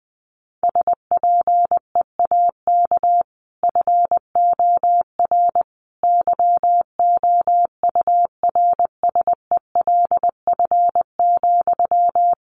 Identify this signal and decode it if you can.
Morse code, heard as a keyed tone, about 20 wpm, 700 Hz. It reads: SPEAK FOR YOURSELF,
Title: Sentences from Top 1100 Words 20wpm